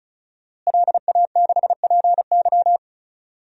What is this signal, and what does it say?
Morse code, heard as a keyed tone, about 34 wpm, 700 Hz. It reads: LA6PY
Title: DXCC-100 Call Signs - 5 Characters - Repeated 2x - 35wpm